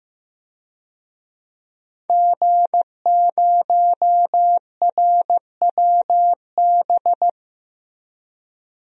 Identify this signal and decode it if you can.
Morse code, heard as a keyed tone, about 15 wpm, 700 Hz. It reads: G0RWB
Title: DXCC-100 Call Signs - 5 Characters - 15wpm